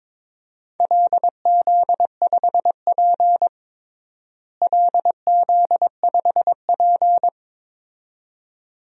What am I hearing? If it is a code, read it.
Morse code, heard as a keyed tone, about 22 wpm, 700 Hz. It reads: LZ5P LZ5P
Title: DXCC-100 Call Signs - 4 Characters - Repeated 2x - 22wpm